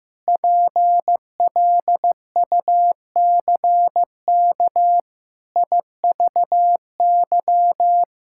Morse code, heard as a keyed tone, about 15 wpm, 700 Hz. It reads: PLUCK IVY
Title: Pangram Challenge v2 - 15wpm